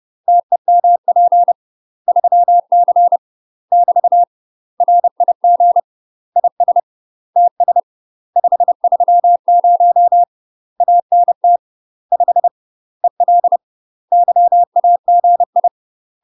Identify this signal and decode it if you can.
Morse code, heard as a keyed tone, about 30 wpm, 700 Hz. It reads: TEMP 3C <BT> RIG IS TS 530 ANT 5 EL YAGI
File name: UK MASTER QSOs TRAINING / 30WPM / UK MASTER QSOs 0044